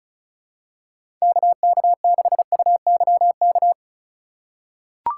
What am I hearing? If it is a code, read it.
Morse code, heard as a keyed tone, about 36 wpm, 700 Hz. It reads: KK6UYK
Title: USA Call Signs - 6 Characters - 2x Character Spacing - 35wpm